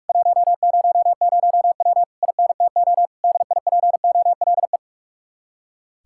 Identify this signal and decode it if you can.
Morse code, heard as a keyed tone, about 58 wpm, 700 Hz. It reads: 100W INTO DIPOLE